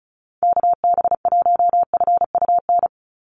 Morse code, heard as a keyed tone, about 34 wpm, 700 Hz. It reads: KB1FUN
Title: USA Call Signs 35wpm